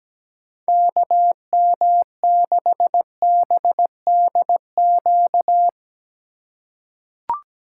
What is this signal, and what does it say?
Morse code, heard as a keyed tone, about 17 wpm, 700 Hz. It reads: KM6BDQ
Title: USA Call Signs - 4x Character Spacing - 17wpm